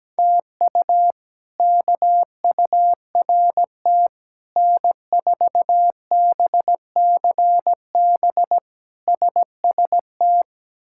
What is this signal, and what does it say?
Morse code, heard as a keyed tone, about 17 wpm, 700 Hz. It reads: TU KURT N4BCB SST
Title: K1USN SST QSO Practice - 17wpm